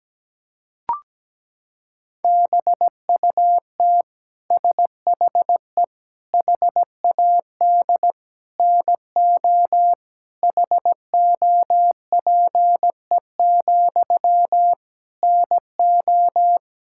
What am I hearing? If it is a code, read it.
Morse code, heard as a keyed tone, about 17 wpm, 700 Hz. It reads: BUT SHE HAD NO HOPE, NO
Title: Sentences from Top 1600 Words 17wpm